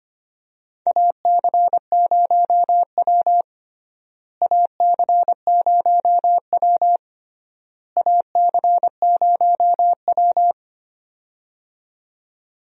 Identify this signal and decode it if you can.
Morse code, heard as a keyed tone, about 25 wpm, 700 Hz. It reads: AC0W AC0W AC0W
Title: USA Call Signs - 4 Characters - Repeated 3x - 25wpm